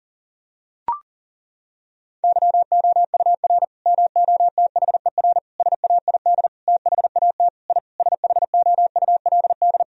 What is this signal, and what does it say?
Morse code, heard as a keyed tone, about 37 wpm, 700 Hz. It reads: YOUR MOTHER SAID THAT I SHOULD
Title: Sentences from Top 1000 Words - Review Part 2 - 40wpm